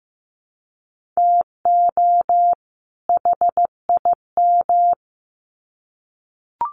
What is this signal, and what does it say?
Morse code, heard as a keyed tone, about 15 wpm, 700 Hz. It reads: TO HIM
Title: Sets of 2 Words 15wpm